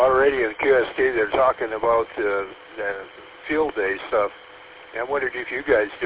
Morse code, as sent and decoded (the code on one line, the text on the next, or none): none